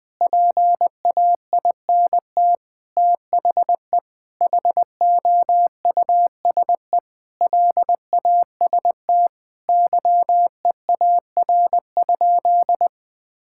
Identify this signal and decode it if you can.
Morse code, heard as a keyed tone, about 21 wpm, 700 Hz. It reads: PAINT THE HOUSE LAST YEAR?
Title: Sentences from Top 1000 Words - Review Part 6 - 20wpm